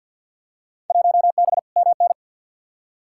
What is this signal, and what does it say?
Morse code, heard as a keyed tone, about 52 wpm, 700 Hz. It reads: 1B MN
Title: Field Day Exchanges Encore - 50wpm